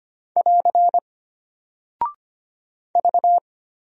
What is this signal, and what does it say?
Morse code, heard as a keyed tone, about 25 wpm, 700 Hz. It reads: <AR> V
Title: Single Character (Letter:Number:Punctuation:Prosign) Alt 25wpm